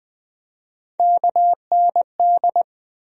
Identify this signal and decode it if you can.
Morse code, heard as a keyed tone, about 20 wpm, 700 Hz. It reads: KND
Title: USA Call Sign Postfixes - 3 Letters 20wpm